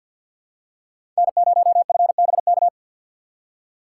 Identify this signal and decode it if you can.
Morse code, heard as a keyed tone, about 51 wpm, 700 Hz. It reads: N0FBK
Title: USA Call Signs - 5 Characters - 2x Character Spacing - 50wpm